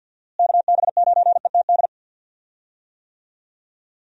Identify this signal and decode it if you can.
Morse code, heard as a keyed tone, about 51 wpm, 700 Hz. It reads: KB9ETD
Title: USA Call Signs - 6 Characters - Repeated 3x - 50wpm